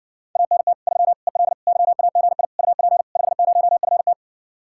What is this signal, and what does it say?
Morse code, heard as a keyed tone, about 60 wpm, 700 Hz. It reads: ANT 3 EL YAGI UP 50FT